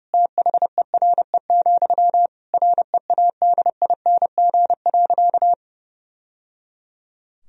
Morse code, heard as a keyed tone, about 30 wpm, 700 Hz. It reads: THERE, READING.
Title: Sentences from Top 1000 Words - Review Part 3 - 30wpm